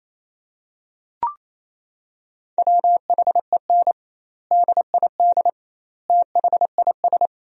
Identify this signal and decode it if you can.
Morse code, heard as a keyed tone, about 28 wpm, 700 Hz. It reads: WHEN DID THIS
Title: 2x Word-Spacing - Sentences from Top 400 Words - 28wpm